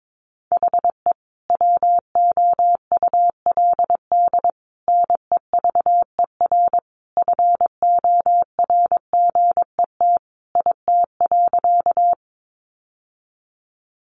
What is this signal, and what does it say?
Morse code, heard as a keyed tone, about 22 wpm, 700 Hz. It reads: HE WOULD NEVER FORGET IT.
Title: Sentences from Top 1200 Words 22wpm